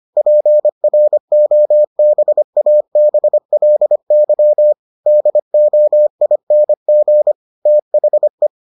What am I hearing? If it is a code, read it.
Morse code, heard as a keyed tone, about 25 wpm, 600 Hz. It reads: PROBABLY DOING THE